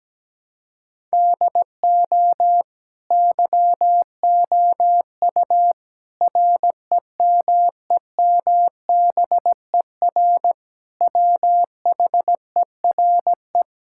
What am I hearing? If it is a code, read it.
Morse code, heard as a keyed tone, about 17 wpm, 700 Hz. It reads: DO YOU REMEMBER WHERE
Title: Sentences from Top 1000 Words - Review Part 6 - 17wpm